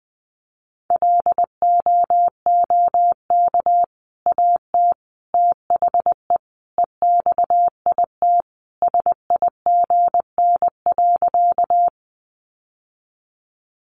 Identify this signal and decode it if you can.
Morse code, heard as a keyed tone, about 20 wpm, 700 Hz. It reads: LOOK AT THE EXIT SIGN.
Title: Sentences from Top 1100 Words 20wpm